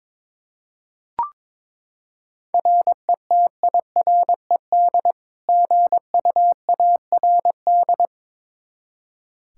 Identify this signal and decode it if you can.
Morse code, heard as a keyed tone, about 22 wpm, 700 Hz. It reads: RETIRED GUARD
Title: Practice QSO - Exchange 3 - More Info - 22wpm